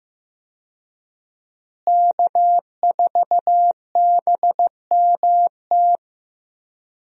Morse code, heard as a keyed tone, about 16 wpm, 700 Hz. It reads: K4BMT
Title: USA Call Signs - 5 Characters - Repeated 3x - 15wpm